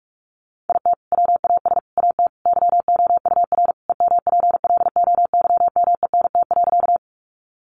Morse code, heard as a keyed tone, about 44 wpm, 700 Hz. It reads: IT WAS AT YOUR EMPLOYMENT.
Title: Sentences from Top 1500 Words 45wpm